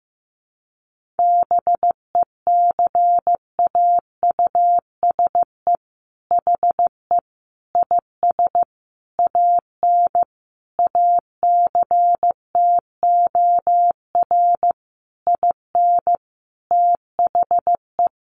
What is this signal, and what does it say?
Morse code, heard as a keyed tone, about 15 wpm, 700 Hz. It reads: BECAUSE HE IS AN ACTOR IN THE
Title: Sentences from Top 1500 Words 15wpm